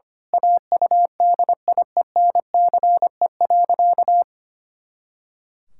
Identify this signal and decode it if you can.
Morse code, heard as a keyed tone, about 25 wpm, 700 Hz. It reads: AUDIENCE.
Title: Pangram Challenge v2 - 25wpm